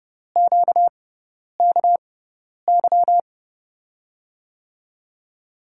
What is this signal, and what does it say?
Morse code, heard as a keyed tone, about 31 wpm, spaced as for 10 wpm, 700 Hz. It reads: QKY
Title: USA Call Signs - 6 Characters - 6x Character Spacing - 30wpm